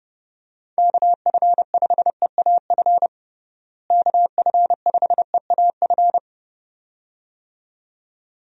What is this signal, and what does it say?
Morse code, heard as a keyed tone, about 30 wpm, 700 Hz. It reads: KF5EAF KF5EAF
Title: USA Call Signs - 6 Characters - Repeated 2x - 30wpm